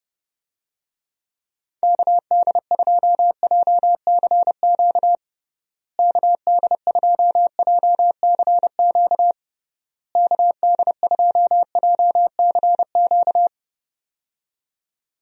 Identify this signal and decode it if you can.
Morse code, heard as a keyed tone, about 30 wpm, 700 Hz. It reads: KD2JCQ KD2JCQ KD2JCQ
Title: USA Call Signs - 6 Characters - Repeated 3x - 30wpm